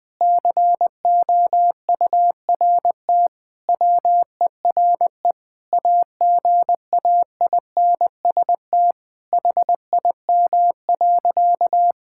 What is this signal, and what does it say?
Morse code, heard as a keyed tone, about 20 wpm, 700 Hz. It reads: COURT WERE AGAINST HIM.
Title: Sentences from Top 1600 Words 20wpm